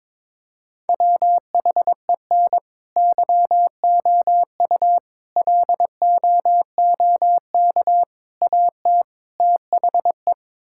Morse code, heard as a keyed tone, about 22 wpm, 700 Hz. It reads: WHEN YOU LOOK AT THE
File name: practice / 22wpm / Build A Phrase - 5-Word Phrases - 22wpm